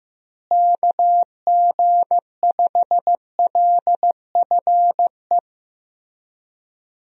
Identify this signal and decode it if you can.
Morse code, heard as a keyed tone, about 16 wpm, 700 Hz. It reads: KG5LFE
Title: USA Call Signs - 6 Characters - Repeated 3x - 15wpm